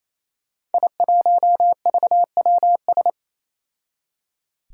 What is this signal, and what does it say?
Morse code, heard as a keyed tone, about 27 wpm, 700 Hz. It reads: I1VWS
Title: Five Challenging Letter-Number Combinations 28wpm